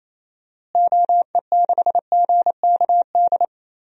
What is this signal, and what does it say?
Morse code, heard as a keyed tone, about 28 wpm, 700 Hz. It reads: OE6GKD
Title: DXCC-100 Call Signs - 6 Characters - Repeated 2x - 28wpm